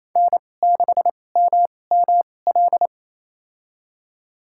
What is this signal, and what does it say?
Morse code, heard as a keyed tone, about 28 wpm, spaced as for 20 wpm, 700 Hz. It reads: N6MML
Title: USA Call Signs - 5 Characters - 2x Character Spacing - 28wpm